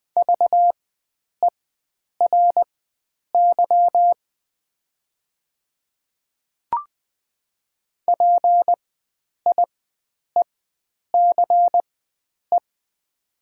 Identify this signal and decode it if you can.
Morse code, heard as a keyed tone, about 20 wpm, spaced as for 9 wpm, 700 Hz. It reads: VERY PIECE
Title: 4x Character Spacing - Top 500 Words - 20wpm